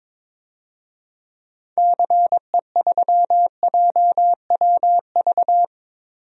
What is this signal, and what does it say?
Morse code, heard as a keyed tone, about 22 wpm, 700 Hz. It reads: CE3JWV
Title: DXCC-100 Call Signs - 6 Characters - 22wpm